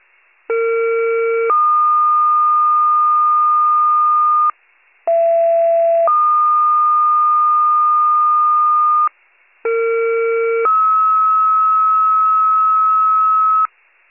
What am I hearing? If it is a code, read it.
Two-tone page, heard as a one-second tone, then a three-second tone, about 460 then 1220 Hz, about 670 then 1220 Hz, about 460 then 1290 Hz. It is heard from 0.5 to 13.7 s.